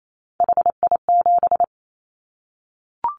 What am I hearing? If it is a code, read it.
Morse code, heard as a keyed tone, about 28 wpm, 700 Hz. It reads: HI7
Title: Callsign Prefixes for Top 100 DXCC - 28wpm